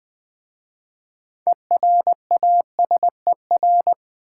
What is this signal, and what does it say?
Morse code, heard as a keyed tone, about 20 wpm, 690 Hz. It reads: ERASER
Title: Easy Words with Hints - 20wpm